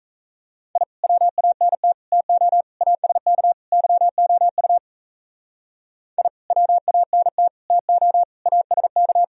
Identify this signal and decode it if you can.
Morse code, heard as a keyed tone, about 35 wpm, 700 Hz. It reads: I WANT TO ASK YOU I WANT TO ASK